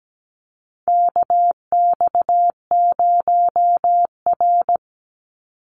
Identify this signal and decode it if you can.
Morse code, heard as a keyed tone, about 17 wpm, 700 Hz. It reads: KX0R
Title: NA SOTA Activator Callsigns 17wpm